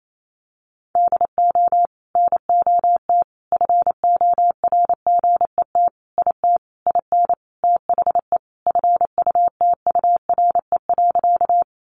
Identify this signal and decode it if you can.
Morse code, heard as a keyed tone, about 28 wpm, 700 Hz. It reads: DO NOT FORGET IT IN THE FUTURE.
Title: Sentences from Top 1200 Words 28wpm